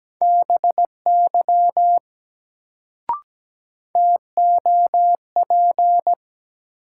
Morse code, heard as a keyed tone, about 17 wpm, 700 Hz. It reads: BY TOP
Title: Sets of 2 Words - New Word Drill - 17wpm